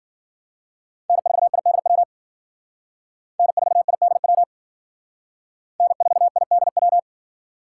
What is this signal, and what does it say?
Morse code, heard as a keyed tone, about 50 wpm, 700 Hz. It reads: N4IDW N4IDW N4IDW